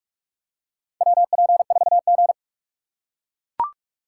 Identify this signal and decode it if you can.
Morse code, heard as a keyed tone, about 43 wpm, 700 Hz. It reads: WP4G
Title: USA Call Signs - 4x Character Spacing - 45wpm